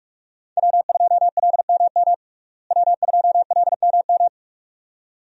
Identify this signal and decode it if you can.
Morse code, heard as a keyed tone, about 43 wpm, 700 Hz. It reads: W2LMM W2LMM
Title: USA Call Signs - 5 Characters - Repeated 2x - 45wpm